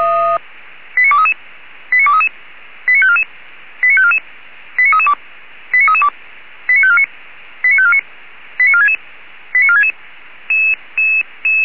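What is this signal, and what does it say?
Five-tone selective call (ZVEI), heard as quick runs of five tones, tones of about 70 ms. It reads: 89233 89233 89544 89544 89332 89332 89549 89549 89466 89466
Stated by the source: ZVEI five-tone selective call received by SDR, code 89384.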